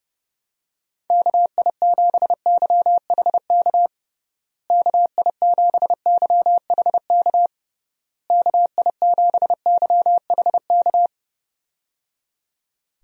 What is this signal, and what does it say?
Morse code, heard as a keyed tone, about 30 wpm, 700 Hz. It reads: KI7YHK KI7YHK KI7YHK
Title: USA Call Signs - 6 Characters - Repeated 3x - 30wpm